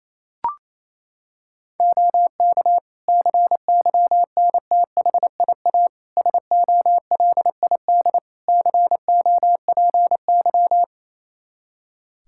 Morse code, heard as a keyed tone, about 28 wpm, 700 Hz. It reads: OK CYNTHIA SOLID COPY
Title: Practice QSO - Exchange 3 - More Info - 28wpm